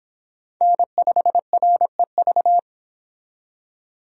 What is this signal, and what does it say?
Morse code, heard as a keyed tone, about 26 wpm, 700 Hz. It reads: N5REV